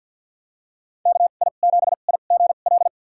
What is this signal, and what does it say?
Morse code, heard as a keyed tone, about 50 wpm, spaced as for 36 wpm, 700 Hz. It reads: KI7IGL
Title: USA Call Signs - 2x Character Spacing - 50wpm